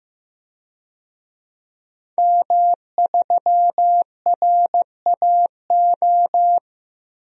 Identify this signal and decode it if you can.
Morse code, heard as a keyed tone, about 15 wpm, 700 Hz. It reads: M3RAO
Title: DXCC-100 Call Signs - 5 Characters - Repeated 3x - 15wpm